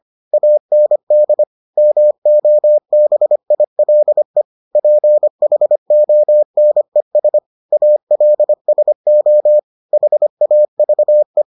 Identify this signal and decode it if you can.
Morse code, heard as a keyed tone, about 25 wpm, 600 Hz. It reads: AND MOBILE PHONES ALSO HAVE